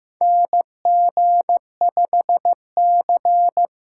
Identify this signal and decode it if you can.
Morse code, heard as a keyed tone, about 15 wpm, 700 Hz. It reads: NG5C
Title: USA Call Signs - 4 Characters - 2x Character Spacing - 15wpm